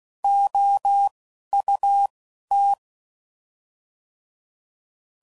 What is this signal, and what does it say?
Morse code, heard as a keyed tone, about 16 wpm, spaced as for 11 wpm, 800 Hz. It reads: OUT
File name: Prefixes / By First Letter / Os